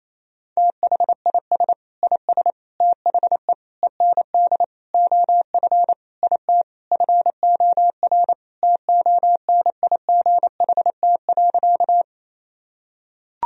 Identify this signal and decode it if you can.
Morse code, heard as a keyed tone, about 27 wpm, 700 Hz. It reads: THIS IS THE END OF IT FOR TONIGHT.
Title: Sentences from Top 1000 Words - Review Part 3 - 28wpm